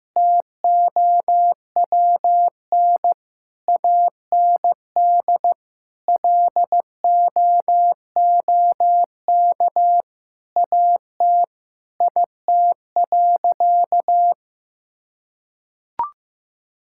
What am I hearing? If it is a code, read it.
Morse code, heard as a keyed tone, about 15 wpm, 700 Hz. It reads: TOWN AND LOOK AT IT.
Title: Sentences from Top 1000 Words - Review Part 5 - 15wpm